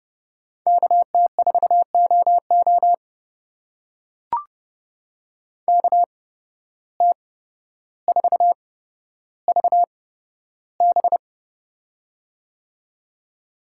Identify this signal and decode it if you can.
Morse code, heard as a keyed tone, about 31 wpm, spaced as for 13 wpm, 700 Hz. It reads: KT4OO KT4VB
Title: USA Call Signs - 5 Characters - 8x Character Spacing - 30wpm